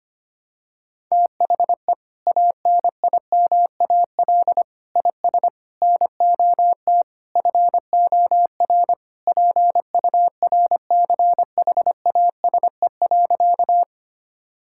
Morse code, heard as a keyed tone, about 25 wpm, 700 Hz. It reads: THE ANIMAL IS NOT FOR PURCHASE.
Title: Sentences from Top 1000 Words - Review Part 5 - 25wpm